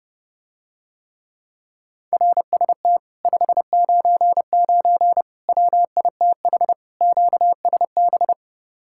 Morse code, heard as a keyed tone, about 30 wpm, 700 Hz. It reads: RST 599 WITH QSB